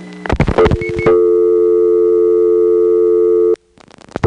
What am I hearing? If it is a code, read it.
Dial tone (North America), 350 + 440 Hz.
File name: GroupBell / MariettaMap-unfinished / Marietta1